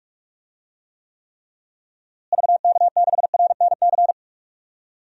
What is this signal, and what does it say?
Morse code, heard as a keyed tone, about 41 wpm, 700 Hz. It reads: VK6RNC
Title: DXCC-100 Call Signs - 6 Characters - Repeated 3x - 45wpm